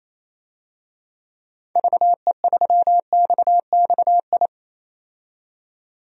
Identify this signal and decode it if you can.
Morse code, heard as a keyed tone, about 28 wpm, 700 Hz. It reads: VE3XXI
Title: DXCC-100 Call Signs - 6 Characters - 28wpm